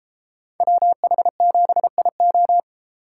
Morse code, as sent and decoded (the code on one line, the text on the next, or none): .-- .... --... .. ---
WH7IO